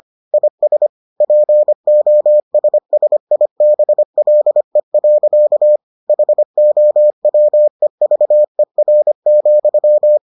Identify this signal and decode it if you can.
Morse code, heard as a keyed tone, about 25 wpm, 600 Hz. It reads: IS POSSIBLE. HOWEVER,